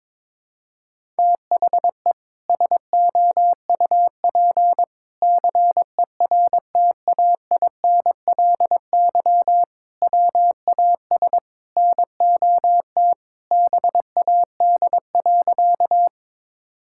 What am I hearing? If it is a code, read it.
Morse code, heard as a keyed tone, about 22 wpm, 700 Hz. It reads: THE SOUP CERTAINLY WAS NOT BAD.
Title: Sentences from Top 1000 Words - Review Part 6 - 22wpm